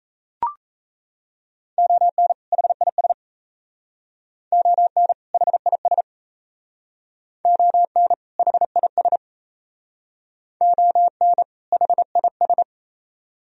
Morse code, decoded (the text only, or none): ON HIS ON HIS ON HIS ON HIS